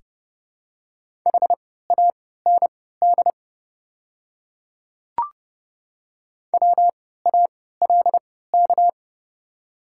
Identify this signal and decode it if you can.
Morse code, heard as a keyed tone, about 30 wpm, spaced as for 17 wpm, 700 Hz. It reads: HAND WALK
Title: 3x Character Spacing - Top 300 Words - 30wpm